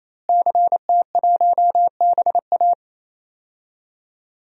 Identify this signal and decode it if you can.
Morse code, heard as a keyed tone, about 28 wpm, 700 Hz. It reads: CT1BA
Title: DXCC-100 Call Signs - 5 Characters - Repeated 3x - 28wpm